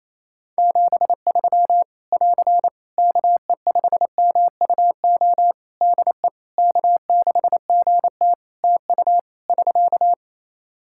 Morse code, heard as a keyed tone, about 28 wpm, 700 Hz. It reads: 73 <AR> KE5MUO DE K6GT TU <SK>
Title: Practice QSO - Exchange 4 - Ending - 28wpm